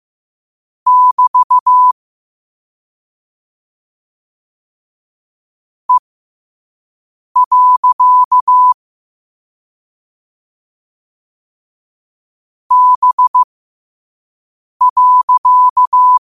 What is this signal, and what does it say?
Morse code, heard as a keyed tone, about 15 wpm, spaced as for 5 wpm, 1000 Hz. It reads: <BT> E. B.
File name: qotdfiles / 5wpm